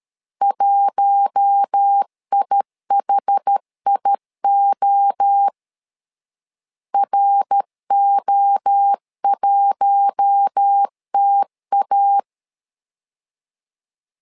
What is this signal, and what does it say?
Morse code, heard as a keyed tone, about 13 wpm, 800 Hz. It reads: 1IHIO RO1TA
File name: LEZIONI / 4 Q1 12wpm 2028